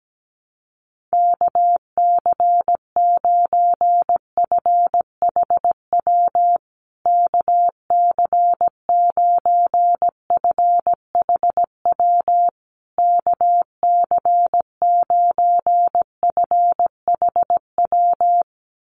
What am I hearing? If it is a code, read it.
Morse code, heard as a keyed tone, about 17 wpm, 700 Hz. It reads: KC9FHW KC9FHW KC9FHW
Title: USA Call Signs Repeated 3x 17wpm